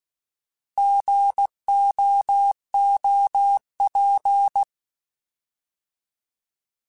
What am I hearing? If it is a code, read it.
Morse code, heard as a keyed tone, about 16 wpm, 800 Hz. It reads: GOOP